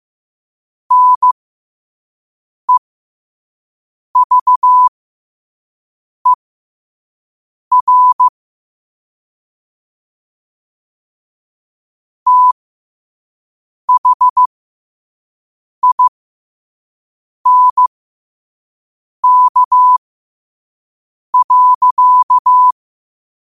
Morse code, heard as a keyed tone, about 15 wpm, spaced as for 5 wpm, 1000 Hz. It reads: NEVER THINK.
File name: qotdfiles / 5wpm